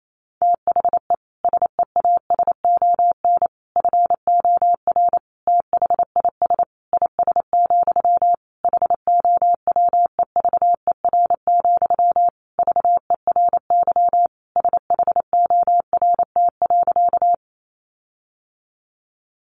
Morse code, heard as a keyed tone, about 28 wpm, 700 Hz. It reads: THE SEASON FOR THIS IS, HOWEVER, VERY SHORT.
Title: Sentences from Top 1400 Words 28wpm